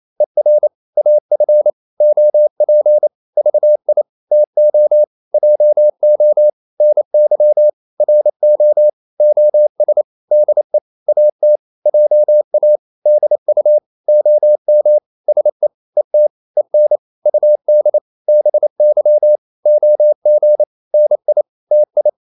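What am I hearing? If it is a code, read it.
Morse code, heard as a keyed tone, about 28 wpm, 600 Hz. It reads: ER AF OP VI TO JO NY RO OS DE AT JA DU OM SE ET EN UD BY OG NI TI
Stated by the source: Danish short words sent in Morse at 28 wpm, from CW med Gnister.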